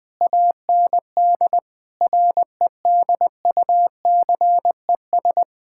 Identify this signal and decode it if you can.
Morse code, heard as a keyed tone, about 21 wpm, 700 Hz. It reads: AND REDUCES
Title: Difficult Sentences with Hints - 20wpm